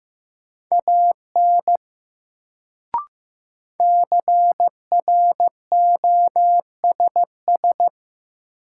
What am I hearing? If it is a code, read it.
Morse code, heard as a keyed tone, about 15 wpm, 700 Hz. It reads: AN CROSS
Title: Sentences from Top 300 Words - New Word Drill - 15wpm